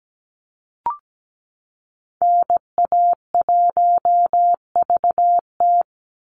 Morse code, heard as a keyed tone, about 17 wpm, 700 Hz. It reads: NA1VT
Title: Call Signs Repeated 2x - 17wpm